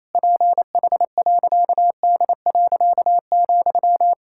Morse code, heard as a keyed tone, about 28 wpm, 700 Hz. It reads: PH.D.,
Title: Pangram Challenge v2 - 28wpm